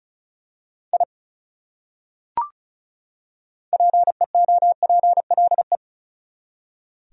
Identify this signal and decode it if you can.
Morse code, heard as a keyed tone, about 36 wpm, 700 Hz. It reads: I PEOPLE
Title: Top 100 Words 35wpm